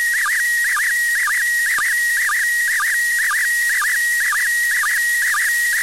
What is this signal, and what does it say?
SSTV (slow-scan television), PD 120.
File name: ISS-SSTV-audio z 20-04-40 13-11-2024